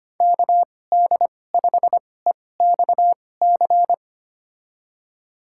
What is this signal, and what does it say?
Morse code, heard as a keyed tone, about 25 wpm, spaced as for 18 wpm, 700 Hz. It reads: KD5EXC
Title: USA Call Signs - 6 Characters - 2x Character Spacing - 25wpm